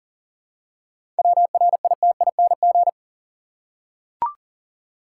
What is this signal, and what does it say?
Morse code, heard as a keyed tone, about 40 wpm, 700 Hz. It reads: WRITING
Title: Seven Letter Words - 40wpm